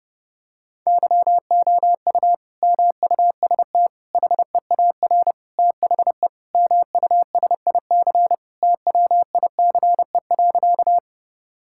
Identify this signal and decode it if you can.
Morse code, heard as a keyed tone, about 30 wpm, 700 Hz. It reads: YOU MUST HEAR THE MUSIC TWICE.
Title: Sentences from Top 1600 Words 30wpm